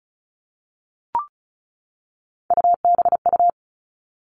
Morse code, heard as a keyed tone, about 34 wpm, 700 Hz. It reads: UBU
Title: All Cut Numbers - 3-Digit 35wpm